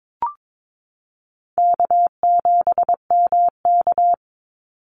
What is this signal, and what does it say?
Morse code, heard as a keyed tone, about 22 wpm, 700 Hz. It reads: K7MK
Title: NA SOTA Activator Callsigns 22wpm